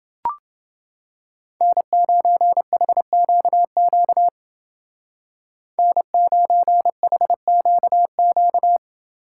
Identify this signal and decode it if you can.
Morse code, heard as a keyed tone, about 29 wpm, 700 Hz. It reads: N9HQQ N9HQQ